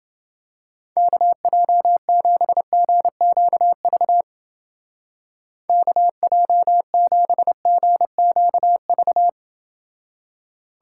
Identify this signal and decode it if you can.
Morse code, heard as a keyed tone, about 29 wpm, 700 Hz. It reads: KJ7GQV KJ7GQV